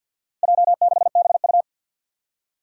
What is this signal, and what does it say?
Morse code, heard as a keyed tone, about 47 wpm, 700 Hz. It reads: J6BU
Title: Four Challenging Letter-Number Combinations 50wpm